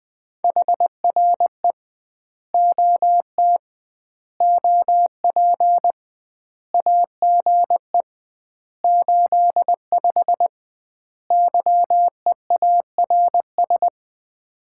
Morse code, heard as a keyed tone, about 20 wpm, 700 Hz. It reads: HRE OT OP AGE 85 YEARS
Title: Common QSO Phrases - 2x Word-Spacing - 20wpm